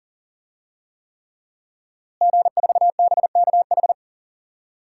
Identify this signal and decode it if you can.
Morse code, heard as a keyed tone, about 37 wpm, 700 Hz. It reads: G4BKH
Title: DXCC-100 Call Signs - 5 Characters - Repeated 3x - 40wpm